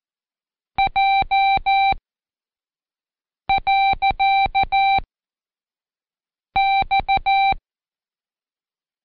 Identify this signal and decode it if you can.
Morse code, heard as a keyed tone, about 14 wpm, 780 Hz. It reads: J . X
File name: Code Quick / AudioDownld1 / Seg65-Day7-Pract Comma